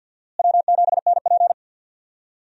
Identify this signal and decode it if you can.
Morse code, heard as a keyed tone, about 46 wpm, 700 Hz. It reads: W7NP
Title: USA Call Signs - 4 Characters - Repeated 3x - 50wpm